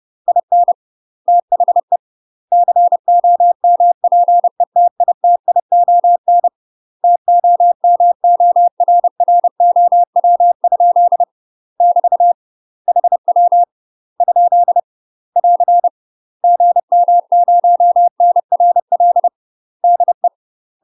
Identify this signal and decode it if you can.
Morse code, heard as a keyed tone, about 30 wpm, 700 Hz. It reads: IN THE COMPETITION TOMORROW? <BT> HW ? <AR> GM0NRL DE
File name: UK MASTER QSOs TRAINING / 30WPM / UK MASTER QSOs 0015